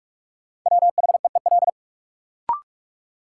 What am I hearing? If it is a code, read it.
Morse code, heard as a keyed tone, about 46 wpm, 700 Hz. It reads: WHEEL
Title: Sentences from Top 500 Words - New Word Drill - 45wpm